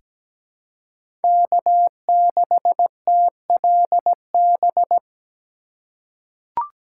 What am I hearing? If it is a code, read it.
Morse code, heard as a keyed tone, about 17 wpm, 700 Hz. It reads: K6TLB
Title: USA Call Signs - 2x Character Spacing - 17wpm